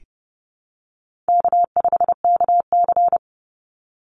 Morse code, heard as a keyed tone, about 30 wpm, 700 Hz. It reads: K5KC
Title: Easy Call Signs - 30wpm